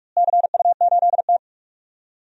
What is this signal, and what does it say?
Morse code, heard as a keyed tone, about 45 wpm, 700 Hz. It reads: CU8T
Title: DXCC-100 Call Signs - 4 Characters - Repeated 3x - 45wpm